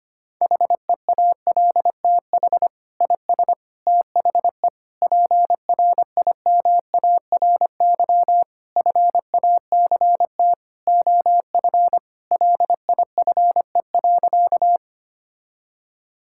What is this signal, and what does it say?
Morse code, heard as a keyed tone, about 25 wpm, 700 Hz. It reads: HEALTH IS THE PRIMARY FACT OF LIFE.
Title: Sentences from Top 1000 Words - Review Part 5 - 25wpm